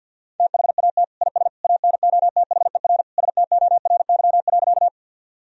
Morse code, heard as a keyed tone, about 53 wpm, 700 Hz. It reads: THAT IS ANOTHER STORY.
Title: Sentences from Top 1000 Words - Review Part 1 - 50wpm